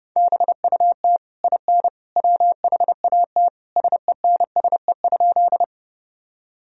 Morse code, heard as a keyed tone, about 30 wpm, 700 Hz. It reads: BUT IN WHAT SENSE?
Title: Sentences from Top 1000 Words - Review Part 4 - 30wpm